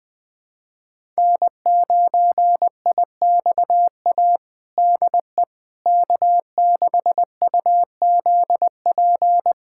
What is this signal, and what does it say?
Morse code, heard as a keyed tone, about 20 wpm, 700 Hz. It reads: N9IXA DE K6UZP
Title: USA Callsign DE Callsign - 5 Characters - 20wpm